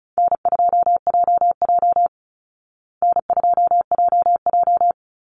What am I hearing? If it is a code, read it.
Morse code, heard as a keyed tone, about 34 wpm, 700 Hz. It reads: N2JJ N2JJ
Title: Easy Call Signs Repeated 2x - 35wpm